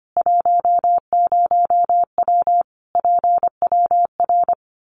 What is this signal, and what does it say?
Morse code, heard as a keyed tone, about 25 wpm, 700 Hz. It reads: 10W PWR
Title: Common QSO Phrases 25wpm